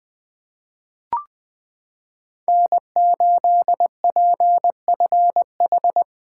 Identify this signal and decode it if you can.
Morse code, heard as a keyed tone, about 20 wpm, 700 Hz. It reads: N8PFH
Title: USA Callsign DE Callsign - 5 Characters - 20wpm